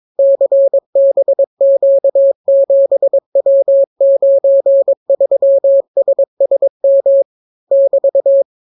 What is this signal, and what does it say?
Morse code, heard as a keyed tone, about 22 wpm, 550 Hz. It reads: CBQ7W93SSM <BT>